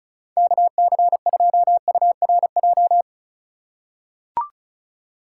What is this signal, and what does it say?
Morse code, heard as a keyed tone, about 36 wpm, 700 Hz. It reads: KC2URJ
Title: USA Call Signs - 6 Characters - 4x Character Spacing - 35wpm